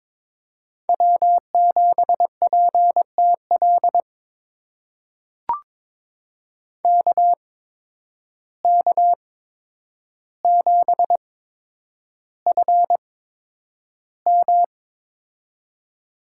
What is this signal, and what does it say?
Morse code, heard as a keyed tone, about 21 wpm, spaced as for 9 wpm, 700 Hz. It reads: W7PTL KK7FM
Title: USA Call Signs - 5 Characters - 8x Character Spacing - 22wpm